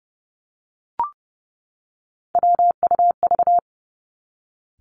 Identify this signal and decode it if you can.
Morse code, heard as a keyed tone, about 30 wpm, 700 Hz. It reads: WUV
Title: Three Challenging Letter-Number Combinations 30wpm